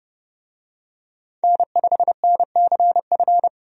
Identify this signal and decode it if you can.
Morse code, heard as a keyed tone, about 32 wpm, 700 Hz. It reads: N5NCF
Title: USA Call Signs - 5 Characters - 30wpm